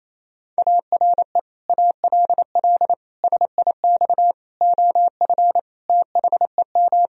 Morse code, heard as a keyed tone, about 28 wpm, 690 Hz. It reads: ARE ALL SIX OF THEM
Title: Sentences from Top 1000 Words - Review Part 2 - 28wpm